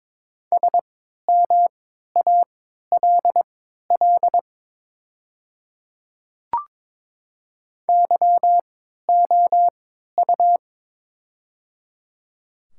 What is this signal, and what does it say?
Morse code, heard as a keyed tone, about 22 wpm, spaced as for 13 wpm, 700 Hz. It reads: SMALL YOU
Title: 3x Character Spacing - Top 200 Words - 22wpm